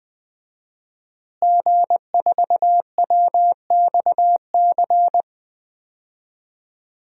Morse code, heard as a keyed tone, about 20 wpm, 700 Hz. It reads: G4WXC
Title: DXCC-100 Call Signs - 5 Characters - Repeated 2x - 20wpm